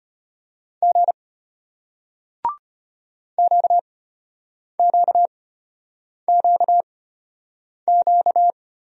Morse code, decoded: G QQQQ

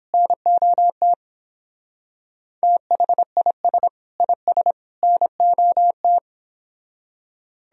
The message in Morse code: NOT THIS IS NOT

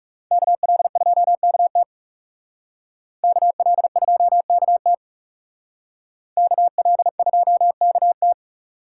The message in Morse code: KL2KT KL2KT KL2KT

